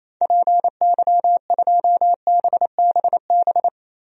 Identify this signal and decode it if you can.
Morse code, heard as a keyed tone, about 28 wpm, 700 Hz. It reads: PY2BBB
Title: DXCC-100 Call Signs - 6 Characters - Repeated 2x - 28wpm